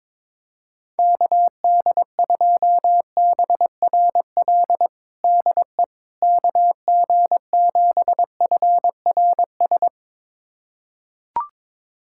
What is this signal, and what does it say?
Morse code, heard as a keyed tone, about 21 wpm, 700 Hz. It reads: KD2BRL DE KG7FRS
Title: USA Callsign DE Callsign 22wpm